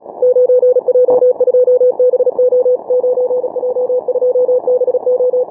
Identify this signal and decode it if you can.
Morse code, heard as a keyed tone, about 37 wpm, 500 Hz. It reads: 9J2BO 9J2BO
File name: training / morserunner / Example